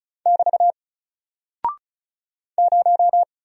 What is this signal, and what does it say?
Morse code, heard as a keyed tone, about 34 wpm, 700 Hz. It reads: <BT> 0
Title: Single Character (Letter:Number:Punctuation:Prosign) 35wpm